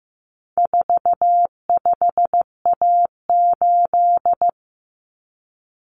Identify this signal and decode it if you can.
Morse code, heard as a keyed tone, about 15 wpm, 700 Hz. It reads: 45A8
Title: Common Cut Numbers - 4-Digit 15wpm